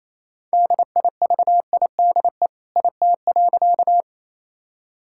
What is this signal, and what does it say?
Morse code, heard as a keyed tone, about 28 wpm, 700 Hz. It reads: DIVIDE IT.
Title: Sentences from Top 1600 Words 28wpm